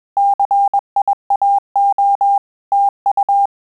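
Morse code, heard as a keyed tone, about 21 wpm, 800 Hz. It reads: CIAO TU
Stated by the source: Morse QSO practice at 20 words per minute, in Italian and English.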